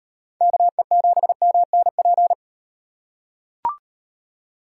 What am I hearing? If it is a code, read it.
Morse code, heard as a keyed tone, about 36 wpm, 700 Hz. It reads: KE7MNP